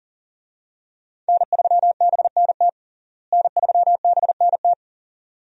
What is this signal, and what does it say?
Morse code, heard as a keyed tone, about 40 wpm, 700 Hz. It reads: N3BNT N3BNT
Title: USA Call Signs - 5 Characters - Repeated 2x - 40wpm